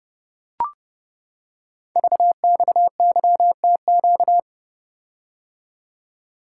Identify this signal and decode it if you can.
Morse code, heard as a keyed tone, about 29 wpm, 700 Hz. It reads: VXYTQ
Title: Code Groups - 5 Letter:Numbers - 30wpm